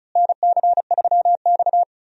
Morse code, heard as a keyed tone, about 36 wpm, 700 Hz. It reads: NC3X
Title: USA Call Signs - 4 Characters - Repeated 2x - 35wpm